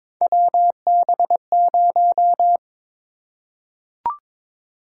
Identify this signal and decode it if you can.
Morse code, heard as a keyed tone, about 22 wpm, 700 Hz. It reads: WB0
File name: practice / 22wpm / Three Letter-Number Combinations - 22wpm